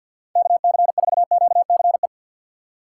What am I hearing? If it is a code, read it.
Morse code, heard as a keyed tone, about 52 wpm, 700 Hz. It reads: KC4QCE
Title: USA Call Signs - 6 Characters - 6x Character Spacing - 50wpm